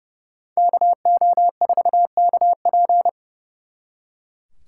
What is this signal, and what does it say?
Morse code, heard as a keyed tone, about 31 wpm, 700 Hz. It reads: KO4KP
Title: Speed Racing - USA 2x2 Call Signs 30wpm